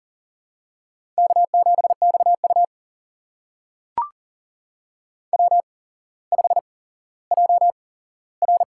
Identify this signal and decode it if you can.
Morse code, heard as a keyed tone, about 42 wpm, spaced as for 17 wpm, 700 Hz. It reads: K7XU W5JR
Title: USA Call Signs - 4 Characters - 8x Character Spacing - 40wpm